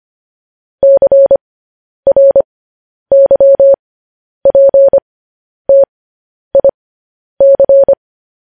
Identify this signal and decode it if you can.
Morse code, heard as a keyed tone, about 25 wpm, spaced as for 10 wpm, 550 Hz. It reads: CRYPTIC
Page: news headlines in morse code at 10 wpm